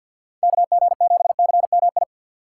Morse code, heard as a keyed tone, about 54 wpm, 700 Hz. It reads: KG7CMI
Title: USA Call Signs - 6 Characters - 8x Character Spacing - 50wpm